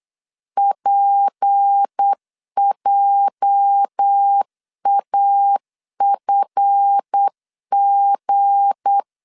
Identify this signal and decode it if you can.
Morse code, heard as a keyed tone, about 8 wpm, 800 Hz. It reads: PJAFG
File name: LEZIONI / 9 2 8wpm 3437